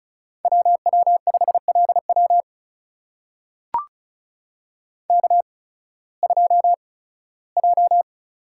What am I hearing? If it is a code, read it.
Morse code, heard as a keyed tone, about 36 wpm, spaced as for 19 wpm, 700 Hz. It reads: WW5LW K2J
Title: USA Call Signs - 5 Characters - 8x Character Spacing - 35wpm